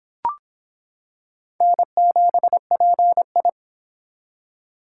Morse code, heard as a keyed tone, about 26 wpm, 700 Hz. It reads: N7PI